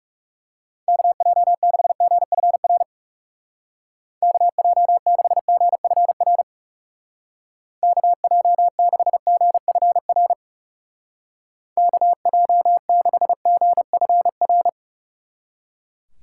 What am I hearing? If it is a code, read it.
Morse code, heard as a keyed tone, about 39 wpm, 700 Hz. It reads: KJ6GFR KJ6GFR KJ6GFR KJ6GFR